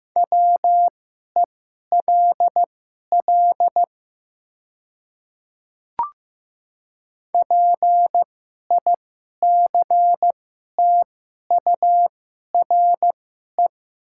Morse code, heard as a keyed tone, about 15 wpm, spaced as for 11 wpm, 700 Hz. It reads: WELL PICTURE
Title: 2x Character Spacing - Top 200 Words - 15wpm